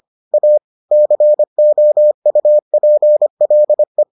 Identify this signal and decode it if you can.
Morse code, heard as a keyed tone, about 25 wpm, 600 Hz. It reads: A COUPLE